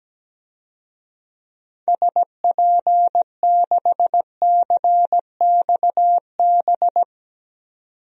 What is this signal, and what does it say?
Morse code, heard as a keyed tone, about 17 wpm, 700 Hz. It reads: SP6CXB
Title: DXCC-100 Call Signs - 6 Characters - 17wpm